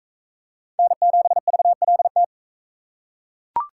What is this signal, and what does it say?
Morse code, heard as a keyed tone, about 37 wpm, 700 Hz. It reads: N7VLT